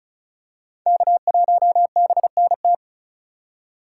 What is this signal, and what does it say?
Morse code, heard as a keyed tone, about 36 wpm, 700 Hz. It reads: K1BNT
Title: USA Call Signs - 5 Characters - 3x Character Spacing - 35wpm